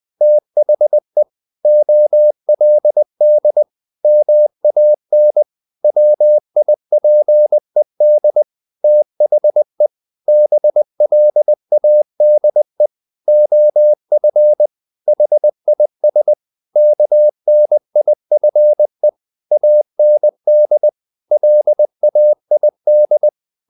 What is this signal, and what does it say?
Morse code, heard as a keyed tone, about 20 wpm, 600 Hz. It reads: THE OLD MAN WIPED THE BLADE OF HIS KNIFE AND LAID
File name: the old man and the sea part 2 20wpm